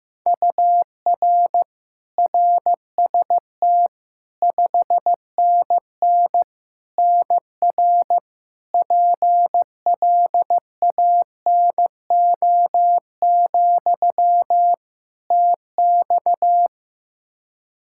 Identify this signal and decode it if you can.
Morse code, heard as a keyed tone, about 15 wpm, 700 Hz. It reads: UR RST 5NN NR PLANO, TX
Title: Practice QSO - Exchange 1 - Beginning - 15wpm